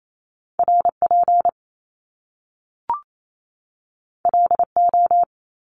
Morse code, heard as a keyed tone, about 28 wpm, 700 Hz. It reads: R P L O
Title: Top Two Letter Combinations in Words 28wpm